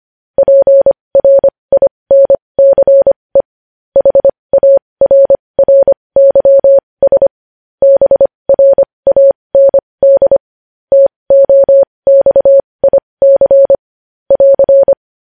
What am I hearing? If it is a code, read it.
Morse code, heard as a keyed tone, about 25 wpm, spaced as for 20 wpm, 550 Hz. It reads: PRINCE HARRYS BRAND TOXIC <AR>
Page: News Headlines in Morse Code at 20 WPM